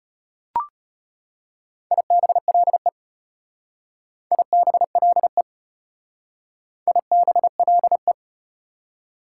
Morse code, heard as a keyed tone, about 32 wpm, 700 Hz. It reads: IBLE IBLE IBLE